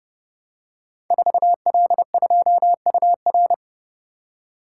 Morse code, heard as a keyed tone, about 30 wpm, 700 Hz. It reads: 4L2UR
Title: Code Groups - 5 Letter:Numbers - 30wpm